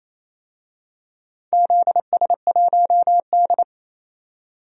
Morse code, heard as a keyed tone, about 28 wpm, 700 Hz. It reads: ZS1D
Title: DXCC-100 Call Signs - 4 Characters - 28wpm